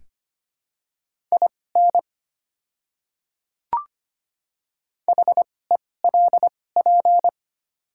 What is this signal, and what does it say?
Morse code, heard as a keyed tone, about 25 wpm, spaced as for 18 wpm, 700 Hz. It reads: IN HELP
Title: 2x Character Spacing - Top 300 Words - 25wpm